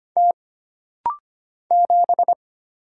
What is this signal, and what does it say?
Morse code, heard as a keyed tone, about 25 wpm, 700 Hz. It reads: T 7
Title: Single Letter-Number Alt - Rapid-Fire 25wpm